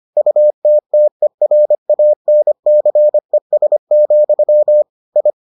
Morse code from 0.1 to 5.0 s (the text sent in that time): UTTERANCES,